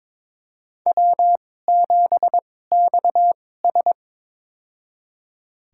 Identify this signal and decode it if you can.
Morse code, heard as a keyed tone, about 22 wpm, spaced as for 16 wpm, 700 Hz. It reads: W7XS
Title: USA Call Signs - 4 Characters - 2x Character Spacing - 22wpm